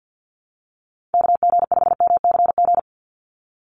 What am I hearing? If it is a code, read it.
Morse code, heard as a keyed tone, about 50 wpm, 700 Hz. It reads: KG5MCG
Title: USA Call Signs 50wpm